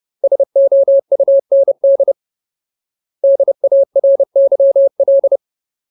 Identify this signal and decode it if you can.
Morse code, heard as a keyed tone, about 30 wpm, 550 Hz. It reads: SOUND DARYL
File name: Ultimate Word List / 30WPM Ultimate